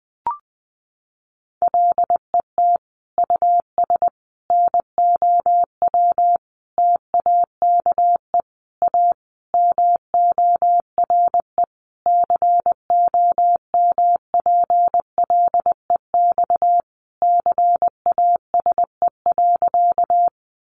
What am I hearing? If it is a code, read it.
Morse code, heard as a keyed tone, about 20 wpm, 700 Hz. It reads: LET US NOW TAKE A MORE COMPLEX CASE.
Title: Sentences from Top 900 Words 20wpm